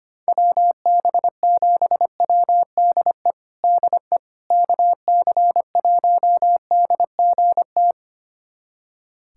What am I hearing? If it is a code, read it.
Morse code, heard as a keyed tone, about 25 wpm, 700 Hz. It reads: WB7WDE DE KC1DGT
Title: USA Callsign DE Callsign - 6 Characters - 25wpm